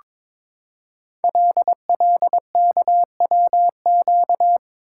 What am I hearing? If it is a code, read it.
Morse code, heard as a keyed tone, about 22 wpm, 700 Hz. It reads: LLKWQ
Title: Code Groups - 5 Letter:Numbers - 22wpm